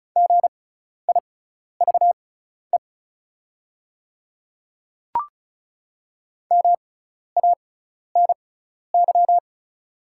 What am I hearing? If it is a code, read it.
Morse code, heard as a keyed tone, about 36 wpm, spaced as for 12 wpm, 690 Hz. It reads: GIVE MANY
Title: 6x Character Spacing - Top 300 Words - 35wpm